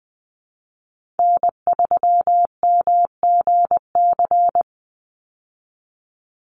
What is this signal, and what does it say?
Morse code, heard as a keyed tone, about 20 wpm, 700 Hz. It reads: N3MGC
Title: USA Call Signs 20wpm